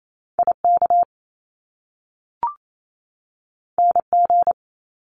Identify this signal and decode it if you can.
Morse code, heard as a keyed tone, about 28 wpm, 700 Hz. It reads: IK NG
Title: Top Two Letter Combinations in Words 28wpm